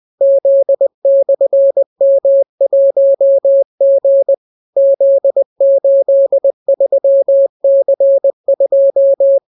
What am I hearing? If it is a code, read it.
Morse code, heard as a keyed tone, about 20 wpm, 550 Hz. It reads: Z/M1G Z83C2